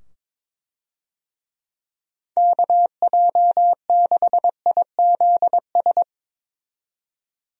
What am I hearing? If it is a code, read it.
Morse code, heard as a keyed tone, about 22 wpm, 700 Hz. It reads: KJ6IZS
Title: USA Call Signs - 6 Characters - 22wpm